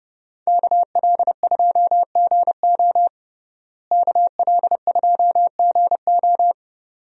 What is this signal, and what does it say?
Morse code, heard as a keyed tone, about 32 wpm, 700 Hz. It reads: KL2GO KL2GO
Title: USA Call Signs - 5 Characters - Repeated 2x - 30wpm